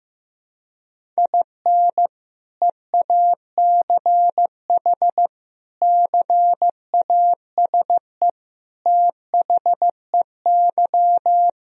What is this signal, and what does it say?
Morse code, heard as a keyed tone, about 15 wpm, 700 Hz. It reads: IN EACH CASE THEY
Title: Sentences from Top 1000 Words - Review Part 1 - 15wpm